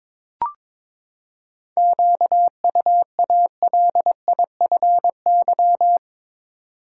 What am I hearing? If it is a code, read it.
Morse code, heard as a keyed tone, about 21 wpm, 700 Hz. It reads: QUALIFY
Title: Seven Letter Words - 22wpm